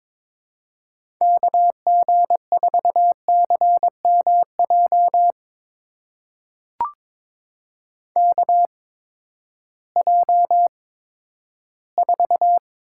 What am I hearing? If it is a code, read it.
Morse code, heard as a keyed tone, about 22 wpm, spaced as for 13 wpm, 700 Hz. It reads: KG4CMJ KJ4
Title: USA Call Signs - 6 Characters - 8x Character Spacing - 22wpm